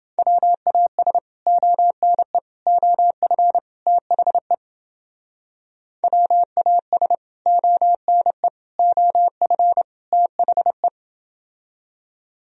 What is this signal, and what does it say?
Morse code, heard as a keyed tone, about 31 wpm, 700 Hz. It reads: WAS ONE OF THE WAS ONE OF THE